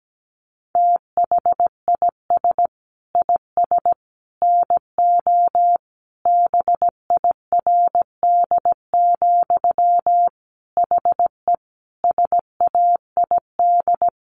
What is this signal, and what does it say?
Morse code, heard as a keyed tone, about 17 wpm, 700 Hz. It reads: THIS IS NO BIRD, HE SAID
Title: Sentences from Top 700 Words 17wpm